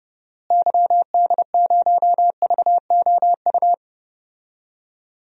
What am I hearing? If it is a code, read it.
Morse code, heard as a keyed tone, about 30 wpm, 700 Hz. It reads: YD0VOU
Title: DXCC-100 Call Signs - 6 Characters - Repeated 3x - 30wpm